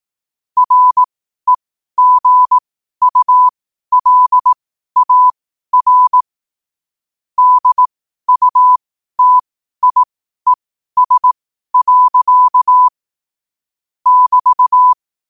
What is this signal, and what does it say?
Morse code, heard as a keyed tone, about 18 wpm, spaced as for 12 wpm, 1000 Hz. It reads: REGULAR DUTIES. <BT>
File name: 12wpm